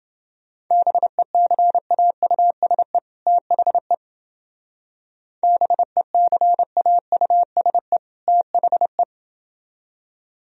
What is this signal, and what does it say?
Morse code, heard as a keyed tone, about 28 wpm, 700 Hz. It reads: BECAUSE THE BECAUSE THE